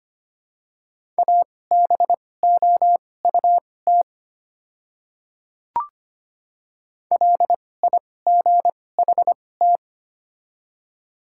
Morse code, heard as a keyed tone, about 25 wpm, spaced as for 18 wpm, 690 Hz. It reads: ABOUT LIGHT
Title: 2x Character Spacing - Top 300 Words - 25wpm